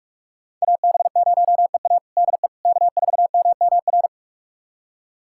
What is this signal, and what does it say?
Morse code, heard as a keyed tone, about 44 wpm, 700 Hz. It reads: AB0EA DE K4MMR
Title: USA Callsign DE Callsign - 5 Characters - 45wpm